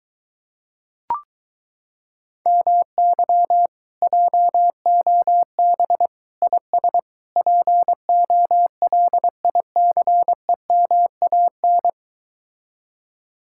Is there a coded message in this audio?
Yes: Morse code from 2.5 to 11.9 s, heard as a keyed tone, about 23 wpm, 700 Hz. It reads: MY JOB IS POLICEMAN